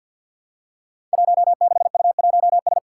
Morse code, heard as a keyed tone, about 46 wpm, 700 Hz. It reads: 16U1S
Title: Five Challenging Letter-Number Combinations 50wpm